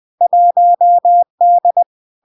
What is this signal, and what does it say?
Morse code, heard as a keyed tone, about 20 wpm, 700 Hz. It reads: 1D
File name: LARK-FD-2022-Log-in-CW-20wpm-700-Hz